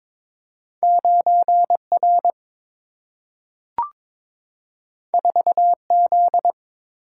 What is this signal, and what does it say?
Morse code, heard as a keyed tone, about 22 wpm, 700 Hz. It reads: 9R 4Z
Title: Single-Letter Single-Number 22wpm